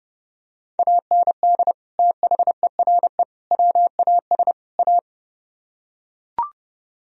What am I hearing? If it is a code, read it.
Morse code, heard as a keyed tone, about 30 wpm, 700 Hz. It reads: AND THERE WAS A